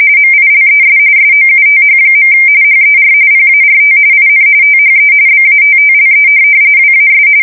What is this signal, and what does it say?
RTTY, about 45.45 baud. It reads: MILES SOUTH. HER PART IN THE BATTLE WAS
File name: fsk170 / USS-New-Jersey